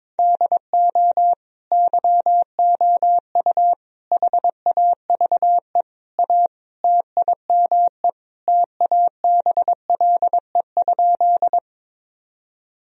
Morse code, heard as a keyed tone, about 22 wpm, 700 Hz. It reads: DO YOU HAVE A TIME TABLE?
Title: Sentences from Top 1000 Words - Review Part 2 - 22wpm